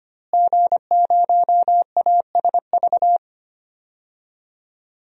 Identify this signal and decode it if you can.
Morse code, heard as a keyed tone, about 25 wpm, 700 Hz. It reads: G0ASV
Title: DXCC-100 Call Signs - 5 Characters - Repeated 3x - 25wpm